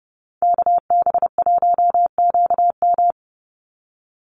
Morse code, heard as a keyed tone, about 30 wpm, 700 Hz. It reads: KB1QM
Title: USA Call Signs 30wpm